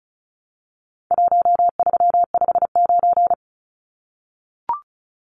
Morse code, heard as a keyed tone, about 34 wpm, 700 Hz. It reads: 1359
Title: Numbers - Contest Style - 4-Digit 35wpm